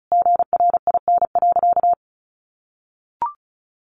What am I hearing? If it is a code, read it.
Morse code, heard as a keyed tone, about 34 wpm, 700 Hz. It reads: GRIN.
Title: A Sound Of Thunder 35wpm